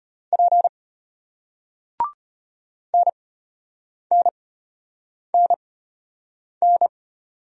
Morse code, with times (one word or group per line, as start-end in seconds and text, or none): 0.3-0.7 P
2.9-6.9 NNNN